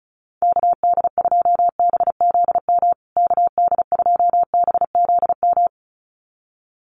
Morse code, heard as a keyed tone, about 34 wpm, 700 Hz. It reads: KD2BZM KD2BZM
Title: USA Call Signs Repeated 2x 35wpm